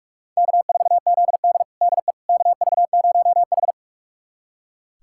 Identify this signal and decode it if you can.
Morse code, heard as a keyed tone, about 46 wpm, 700 Hz. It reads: K4ZD DE KV0H
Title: USA Callsign DE Callsign - 4 Characters - 45wpm